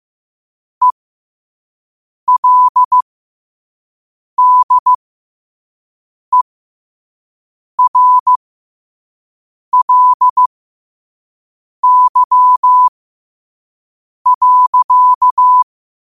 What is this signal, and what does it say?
Morse code, heard as a keyed tone, about 15 wpm, spaced as for 5 wpm, 1000 Hz. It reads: ELDERLY.
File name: qotdfiles / 5wpm